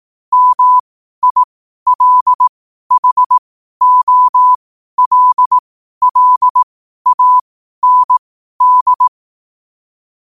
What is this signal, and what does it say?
Morse code, heard as a keyed tone, about 18 wpm, spaced as for 12 wpm, 1000 Hz. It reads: MILHOLLAND
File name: qotdfiles / 12wpm